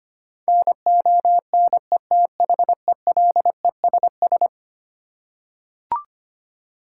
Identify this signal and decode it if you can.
Morse code, heard as a keyed tone, about 25 wpm, 700 Hz. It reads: NONETHELESS
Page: morse-code-ninja-practice-25wpm